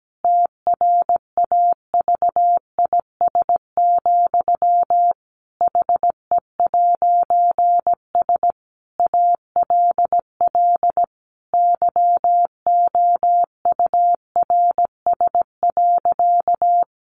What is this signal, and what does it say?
Morse code, heard as a keyed tone, about 17 wpm, 700 Hz. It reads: TRAVIS, HE'S ALL YOURS.
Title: A Sound Of Thunder 17wpm